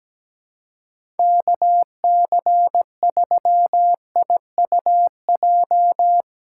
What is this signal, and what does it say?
Morse code, heard as a keyed tone, about 17 wpm, 700 Hz. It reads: KC3IUJ
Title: USA Call Signs - 6 Characters - 4x Character Spacing - 17wpm